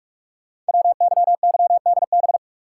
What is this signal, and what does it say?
Morse code, heard as a keyed tone, about 44 wpm, 700 Hz. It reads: WYYDB
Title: Code Groups - 5 Letters - 45wpm